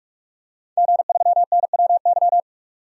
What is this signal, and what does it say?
Morse code, heard as a keyed tone, about 42 wpm, 700 Hz. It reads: G3NWY
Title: DXCC-100 Call Signs - 5 Characters - 45wpm